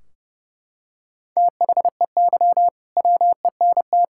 Morse code, heard as a keyed tone, about 32 wpm, 700 Hz. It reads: THEY WENT
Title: Sentences from Top 1600 Words 30wpm